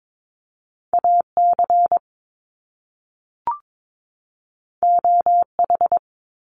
Morse code, heard as a keyed tone, about 22 wpm, 700 Hz. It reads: AC OH
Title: Top Two Letter Combinations in Words 22wpm